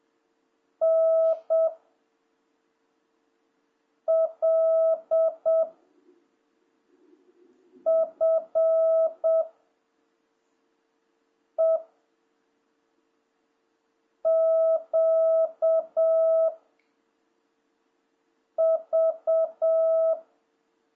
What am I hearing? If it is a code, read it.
Morse code, heard as a keyed tone, about 7 wpm, 660 Hz. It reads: N L F E Q V
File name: jh8cbh / 2026cw